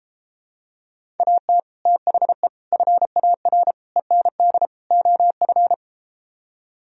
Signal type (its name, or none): Morse code